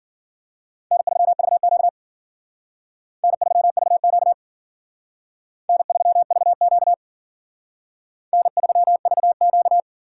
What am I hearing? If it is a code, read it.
Morse code, heard as a keyed tone, about 52 wpm, 700 Hz. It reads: N3VQ N3VQ N3VQ N3VQ